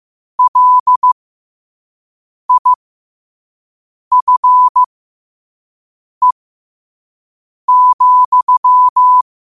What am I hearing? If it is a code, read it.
Morse code, heard as a keyed tone, about 15 wpm, spaced as for 5 wpm, 1000 Hz. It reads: LIFE,
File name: qotdfiles / 5wpm